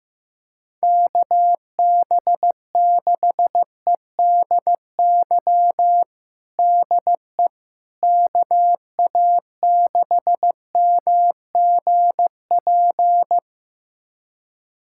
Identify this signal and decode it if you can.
Morse code, heard as a keyed tone, about 15 wpm, 700 Hz. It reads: KB6EDY DE KA6MGP
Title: USA Callsign DE Callsign - 6 Characters - 15wpm